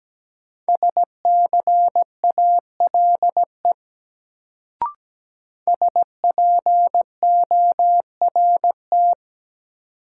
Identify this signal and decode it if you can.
Morse code, heard as a keyed tone, about 17 wpm, 700 Hz. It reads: SCALE SPORT
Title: Sentences from Top 1000 Words - Review Part 4 - New Word Drill - 17wpm